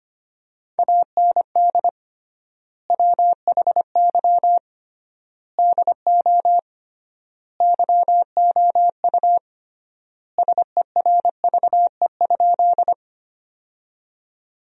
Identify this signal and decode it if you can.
Morse code, heard as a keyed tone, about 25 wpm, 700 Hz. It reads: AND WHY DO YOU SERVE?
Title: 3x Word-Spacing - Sentences from Top 400 Words - 25wpm